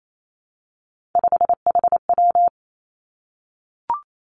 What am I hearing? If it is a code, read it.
Morse code, heard as a keyed tone, about 28 wpm, 690 Hz. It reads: 5HW
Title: Three Challenging Letter-Number Combinations 28wpm